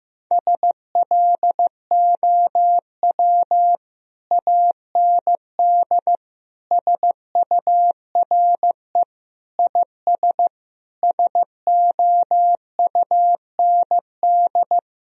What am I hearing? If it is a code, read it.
Morse code, heard as a keyed tone, about 16 wpm, 700 Hz. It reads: SLOW AND SURE IS SOUND
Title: Sentences from Top 1000 Words - Review Part 2 - 15wpm